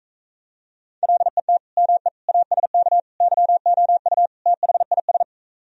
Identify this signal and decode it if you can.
Morse code, heard as a keyed tone, about 43 wpm, 700 Hz. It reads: LET ME ASK YOU THIS